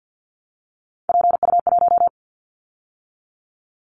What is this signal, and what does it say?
Morse code, heard as a keyed tone, about 50 wpm, 700 Hz. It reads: PU1
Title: Callsign Prefixes for Top 50 DXCC - 50wpm